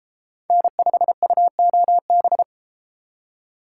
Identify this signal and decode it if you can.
Morse code, heard as a keyed tone, about 30 wpm, 700 Hz. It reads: N5UOB